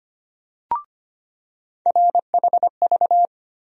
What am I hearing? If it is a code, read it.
Morse code, heard as a keyed tone, about 25 wpm, 700 Hz. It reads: RHV
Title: USA Call Sign Postfixes - 3 Letters 25wpm